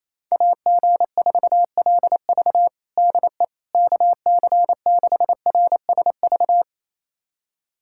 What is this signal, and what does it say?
Morse code, heard as a keyed tone, about 28 wpm, 700 Hz. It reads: AG4LV DE KC6RSV
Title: USA Callsign DE Callsign 28wpm